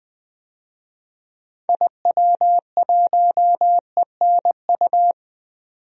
Morse code, heard as a keyed tone, about 20 wpm, 700 Hz. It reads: IW1ENU
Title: DXCC-100 Call Signs - 6 Characters - 20wpm